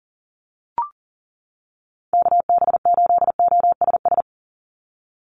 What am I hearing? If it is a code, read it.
Morse code, heard as a keyed tone, about 37 wpm, 690 Hz. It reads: KB8OSS
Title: USA Call Signs 40wpm